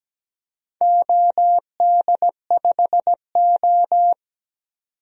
Morse code, heard as a keyed tone, about 17 wpm, 700 Hz. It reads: OD5O
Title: DXCC-100 Call Signs - 4 Characters - 17wpm